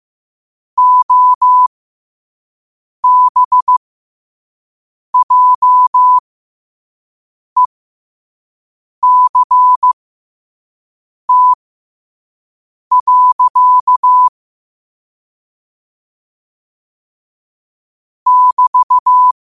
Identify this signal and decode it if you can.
Morse code, heard as a keyed tone, about 15 wpm, spaced as for 5 wpm, 1000 Hz. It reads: OBJECT. <BT>
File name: qotdfiles / 5wpm